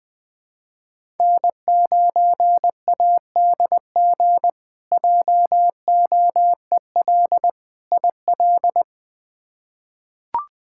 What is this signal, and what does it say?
Morse code, heard as a keyed tone, about 20 wpm, 700 Hz. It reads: N9ADG JOEL IL
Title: K1USN SST QSO Practice - 20wpm